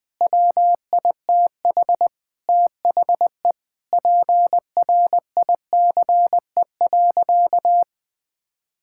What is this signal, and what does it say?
Morse code, heard as a keyed tone, about 20 wpm, 700 Hz. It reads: WITH THE PRICE.
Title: Sentences with 7-Word Phrases - 20wpm